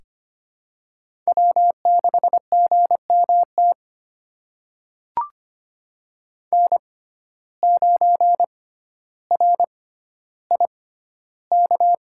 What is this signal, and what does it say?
Morse code, heard as a keyed tone, about 25 wpm, spaced as for 13 wpm, 700 Hz. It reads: W6GMT N9RIK
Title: USA Call Signs - 6x Character Spacing - 25wpm